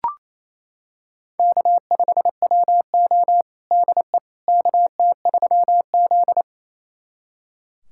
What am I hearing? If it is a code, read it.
Morse code, heard as a keyed tone, about 27 wpm, 700 Hz. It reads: K5WO DE KT3Z
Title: USA Callsign DE Callsign - 4 Characters - 28wpm